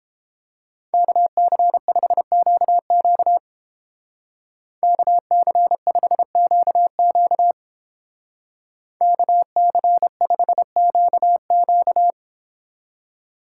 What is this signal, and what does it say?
Morse code, heard as a keyed tone, about 30 wpm, 700 Hz. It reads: KC5QQ KC5QQ KC5QQ